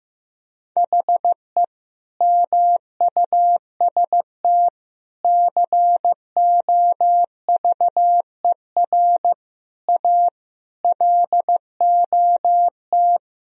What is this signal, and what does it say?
Morse code, heard as a keyed tone, about 15 wpm, 700 Hz. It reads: HE MUST COVER A LOT
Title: Sentences from Top 500 Words and Abbreviations 15wpm